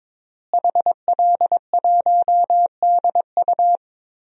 Morse code, heard as a keyed tone, about 21 wpm, 700 Hz. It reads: HL1DU
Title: DXCC-100 Call Signs - 5 Characters - Repeated 2x - 22wpm